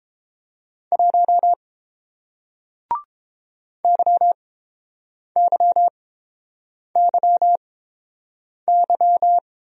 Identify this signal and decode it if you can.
Morse code, heard as a keyed tone, about 30 wpm, spaced as for 7 wpm, 700 Hz. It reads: 1 YYYY